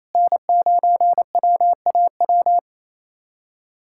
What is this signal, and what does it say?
Morse code, heard as a keyed tone, about 28 wpm, 700 Hz. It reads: N9WAW
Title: USA Callsign DE Callsign 28wpm